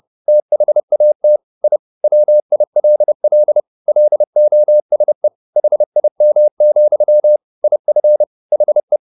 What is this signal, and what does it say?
Morse code, heard as a keyed tone, about 30 wpm, 600 Hz. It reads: THAT I WILL LOSE HIM, IF HE